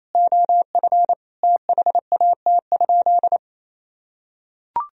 Morse code, heard as a keyed tone, about 28 wpm, 700 Hz. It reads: OF THAT?
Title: Sentences from Top 1000 Words - Review Part 3 - 28wpm